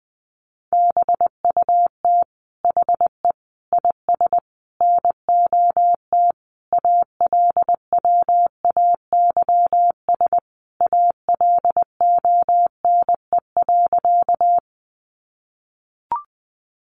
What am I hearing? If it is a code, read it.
Morse code, heard as a keyed tone, about 19 wpm, 700 Hz. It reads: BUT HE IS NOT ALWAYS ALONE.
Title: Sentences from Top 1000 Words 20wpm